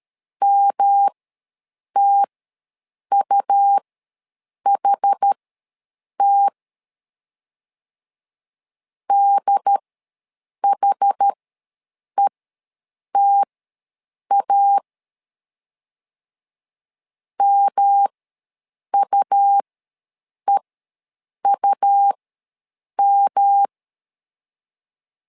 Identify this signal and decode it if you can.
Morse code, heard as a keyed tone, about 13 wpm, spaced as for 7 wpm, 800 Hz. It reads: MTUHT DHETA MUEUM
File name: LEZIONI / 3 MWRO / 3 M ws 12wpm 1745